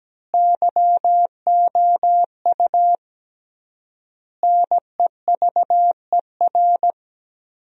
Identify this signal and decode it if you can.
Morse code, heard as a keyed tone, about 17 wpm, 700 Hz. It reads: YOU NEVER
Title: 3x Word-Spacing - Sentences from Top 300 Words - 17wpm